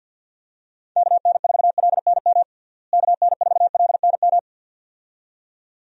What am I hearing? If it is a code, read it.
Morse code, heard as a keyed tone, about 52 wpm, 700 Hz. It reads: KN4LNM KN4LNM
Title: USA Call Signs - 6 Characters - Repeated 2x - 50wpm